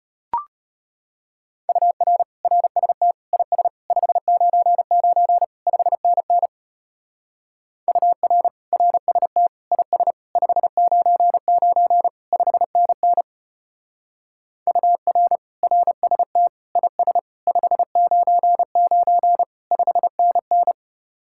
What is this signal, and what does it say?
Morse code, heard as a keyed tone, about 33 wpm, 700 Hz. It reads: UR RST IS 599 5NN UR RST IS 599 5NN UR RST IS 599 5NN